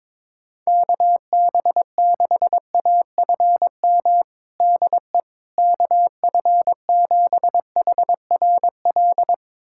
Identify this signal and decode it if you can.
Morse code, heard as a keyed tone, about 22 wpm, 700 Hz. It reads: KB6AFM DE KF7HRL
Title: USA Callsign DE Callsign - 6 Characters - 22wpm